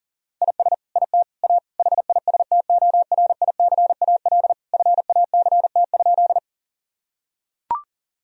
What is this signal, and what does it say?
Morse code, heard as a keyed tone, about 37 wpm, 690 Hz. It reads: IS IT A HISTORICAL FACT?
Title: Sentences from Top 1000 Words - Review Part 4 - 40wpm